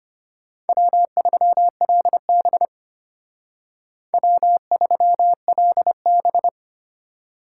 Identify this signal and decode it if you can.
Morse code, heard as a keyed tone, about 32 wpm, 700 Hz. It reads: W3LB W3LB